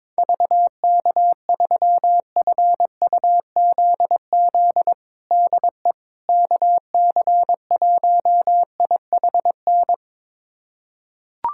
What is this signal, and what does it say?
Morse code, heard as a keyed tone, about 22 wpm, 700 Hz. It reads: VK3FUZZ DE KC1IHN
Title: USA Callsign DE Callsign 22wpm